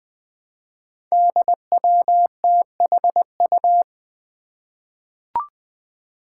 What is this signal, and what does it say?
Morse code, heard as a keyed tone, about 21 wpm, 700 Hz. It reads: DWTHU
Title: Five Challenging Letter-Number Combinations 20wpm